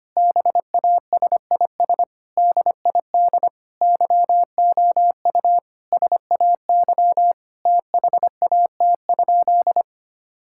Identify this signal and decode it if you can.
Morse code, heard as a keyed tone, about 25 wpm, 700 Hz. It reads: BASIS DID YOU SAY THAT?
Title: Sentences from Top 1000 Words - Review Part 5 - 25wpm